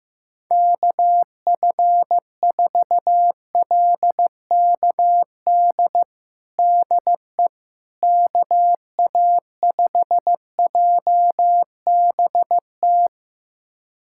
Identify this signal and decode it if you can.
Morse code, heard as a keyed tone, about 15 wpm, 700 Hz. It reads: KF4LKD DE KA5JBT
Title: USA Callsign DE Callsign - 6 Characters - 15wpm